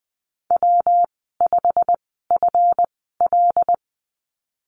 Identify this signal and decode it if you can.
Morse code, heard as a keyed tone, about 20 wpm, spaced as for 15 wpm, 700 Hz. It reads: W5FL
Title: 140 Lesson 23g - Letter f - Call Signs 20wpm